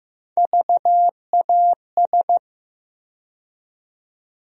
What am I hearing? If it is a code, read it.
Morse code, heard as a keyed tone, about 15 wpm, 700 Hz. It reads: VAS
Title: USA Call Sign Postfixes - 3 Letters 15wpm